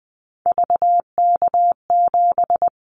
Morse code, heard as a keyed tone, about 20 wpm, 700 Hz. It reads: VK7
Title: Callsign Prefixes for Top 100 DXCC - 20wpm